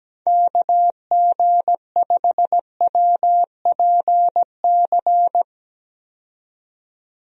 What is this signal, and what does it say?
Morse code, heard as a keyed tone, about 17 wpm, 700 Hz. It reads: KG5WPC
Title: USA Call Signs - 6 Characters - 17wpm